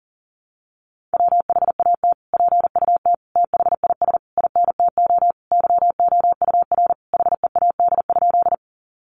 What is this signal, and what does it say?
Morse code, heard as a keyed tone, about 40 wpm, 700 Hz. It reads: WHAT PUT THIS INTO YOUR HEAD?
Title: Sentences from Top 600 Words 40wpm